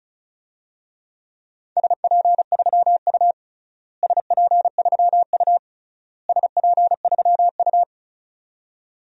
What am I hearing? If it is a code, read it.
Morse code, heard as a keyed tone, about 34 wpm, 700 Hz. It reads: SP3U SP3U SP3U
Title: DXCC-100 Call Signs - 4 Characters - Repeated 3x - 35wpm